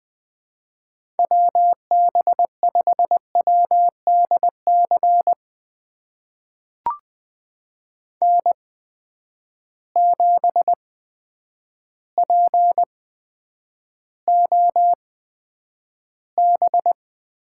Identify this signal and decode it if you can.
Morse code, heard as a keyed tone, about 20 wpm, spaced as for 9 wpm, 700 Hz. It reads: WB5WDC N7POB
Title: USA Call Signs - 8x Character Spacing - 20wpm